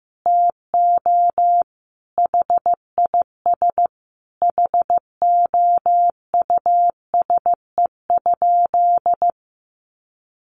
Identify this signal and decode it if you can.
Morse code, heard as a keyed tone, about 14 wpm, 700 Hz. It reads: TO HIS HOUSE?
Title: Sentences from Top 600 Words 15wpm